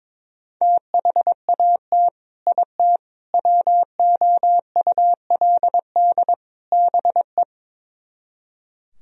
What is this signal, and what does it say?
Morse code, heard as a keyed tone, about 22 wpm, 700 Hz. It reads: THAT IT WOULD BE
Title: Speed-Racing - Sets of 4 Words 22wpm